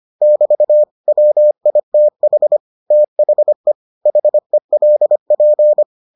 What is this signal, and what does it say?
Morse code, heard as a keyed tone, about 25 wpm, 600 Hz. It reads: <BT> WITH THE HELP